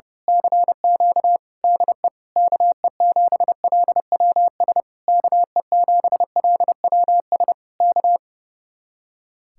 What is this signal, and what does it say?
Morse code, heard as a keyed tone, about 29 wpm, 700 Hz. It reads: CQ DE KE7LWS KE7LWS K
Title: Speed-Racing Common QSO Phrases - 30wpm